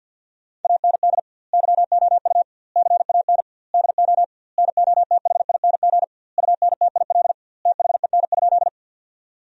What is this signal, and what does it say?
Morse code, heard as a keyed tone, about 53 wpm, 700 Hz. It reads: AND YOU CAN DO NOTHING UNTIL THEN?
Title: 2x Word-Spacing - Sentences from Top 500 Words - 50wpm